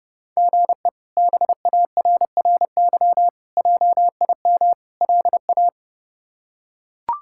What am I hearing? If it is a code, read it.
Morse code, heard as a keyed tone, about 32 wpm, 700 Hz. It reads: GE BARRY JIM LA
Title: K1USN SST QSO Practice - 30wpm